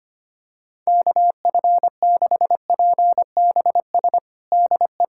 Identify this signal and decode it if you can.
Morse code, heard as a keyed tone, about 25 wpm, 700 Hz. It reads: KF6PBS DE
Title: USA Callsign DE Callsign - 6 Characters - 25wpm